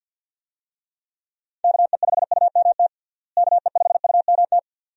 Morse code, heard as a keyed tone, about 48 wpm, 700 Hz. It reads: KE5UMT KE5UMT
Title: USA Call Signs - 6 Characters - Repeated 2x - 50wpm